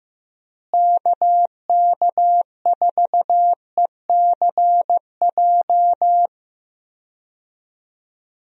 Morse code, heard as a keyed tone, about 15 wpm, 700 Hz. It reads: KK4ECJ
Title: USA Call Signs - 6 Characters - Repeated 3x - 15wpm